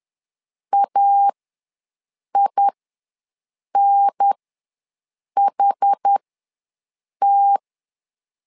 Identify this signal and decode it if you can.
Morse code, heard as a keyed tone, about 10 wpm, spaced as for 6 wpm, 800 Hz. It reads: AINHT
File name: 2 UHDN ws 10wpm